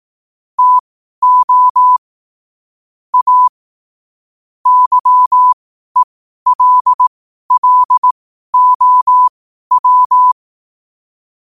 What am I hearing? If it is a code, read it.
Morse code, heard as a keyed tone, about 18 wpm, spaced as for 12 wpm, 1000 Hz. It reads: TO A YELLOW
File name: qotdfiles / 12wpm